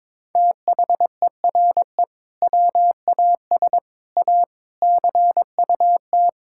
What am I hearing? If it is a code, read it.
Morse code, heard as a keyed tone, about 22 wpm, 700 Hz. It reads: THERE WAS A CUT
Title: Sentences from Top 1000 Words - Review Part 5 - 22wpm